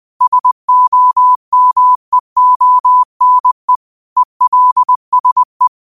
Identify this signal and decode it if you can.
Morse code, heard as a keyed tone, about 20 wpm, 1000 Hz. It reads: SOMEONE ELSE
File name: qotdfiles / 20wpm